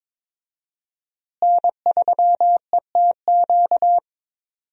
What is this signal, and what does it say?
Morse code, heard as a keyed tone, about 22 wpm, 700 Hz. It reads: N3ETQ
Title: USA Call Signs - 5 Characters - Repeated 2x - 22wpm